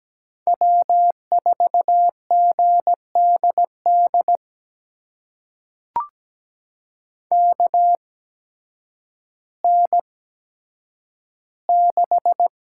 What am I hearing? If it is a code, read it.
Morse code, heard as a keyed tone, about 17 wpm, spaced as for 9 wpm, 700 Hz. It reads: W4GDD KN6
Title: USA Call Signs - 5 Characters - 8x Character Spacing - 17wpm